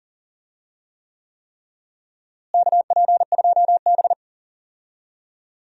Morse code, heard as a keyed tone, about 37 wpm, 700 Hz. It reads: KP2B
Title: DXCC-100 Call Signs - 4 Characters - Repeated 3x - 40wpm